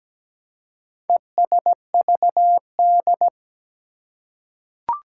Morse code, heard as a keyed tone, about 17 wpm, 700 Hz. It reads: ESVD
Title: Four Challenging Letter-Number Combinations 17wpm